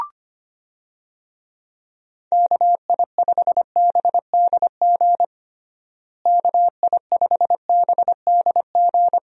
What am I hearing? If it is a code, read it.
Morse code, heard as a keyed tone, about 25 wpm, 700 Hz. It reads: KI5BDG KI5BDG
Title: USA Call Signs - 6 Characters - Repeated 2x - 25wpm